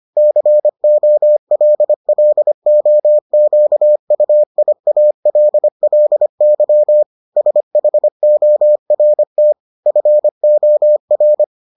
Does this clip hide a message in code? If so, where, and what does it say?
Yes: Morse code from 0.2 to 11.4 s, heard as a keyed tone, about 25 wpm, 600 Hz. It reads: COLLOQUIALLY SHORT FOR